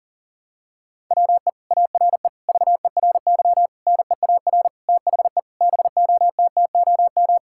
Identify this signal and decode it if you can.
Morse code, heard as a keyed tone, about 40 wpm, 700 Hz. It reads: WE ARE VERY NEAR THE BOTTOM
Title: Sentences from Top 1000 Words - Review Part 6 - 40wpm